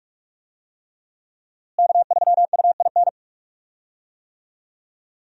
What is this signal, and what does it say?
Morse code, heard as a keyed tone, about 45 wpm, 700 Hz. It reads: K3UIN
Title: USA Call Signs - 5 Characters - Repeated 3x - 45wpm